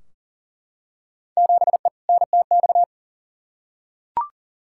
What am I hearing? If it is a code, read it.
Morse code, heard as a keyed tone, about 39 wpm, 700 Hz. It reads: 7E NTX
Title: Field Day Exchanges Encore - 40wpm